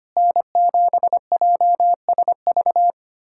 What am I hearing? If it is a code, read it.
Morse code, heard as a keyed tone, about 25 wpm, 700 Hz. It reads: N7JSV
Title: USA Callsign DE Callsign - 5 Characters - 25wpm